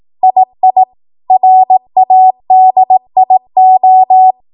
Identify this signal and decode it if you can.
Morse code, heard as a keyed tone, about 18 wpm, 750 Hz. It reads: II RADIO